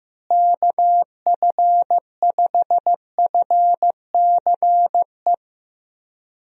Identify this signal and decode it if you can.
Morse code, heard as a keyed tone, about 15 wpm, 700 Hz. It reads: KF5FCE
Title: USA Call Signs - 6 Characters - Repeated 2x - 15wpm